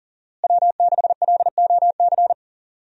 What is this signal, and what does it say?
Morse code, heard as a keyed tone, about 40 wpm, 700 Hz. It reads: W6LOC
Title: USA Call Signs - 5 Characters - Repeated 2x - 40wpm